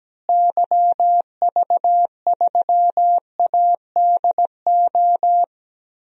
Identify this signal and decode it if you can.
Morse code, heard as a keyed tone, about 17 wpm, 700 Hz. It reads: YV3ADO
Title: DXCC-100 Call Signs - 6 Characters - Repeated 3x - 17wpm